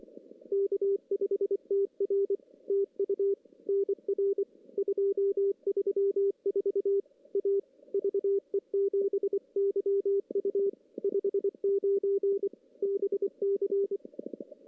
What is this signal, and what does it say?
Morse code, heard as a keyed tone, about 24 wpm, 400 Hz. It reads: K5TR TU NR 234 A VE7YU 59 BC
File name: contests / CW-SS-k5tr- -1600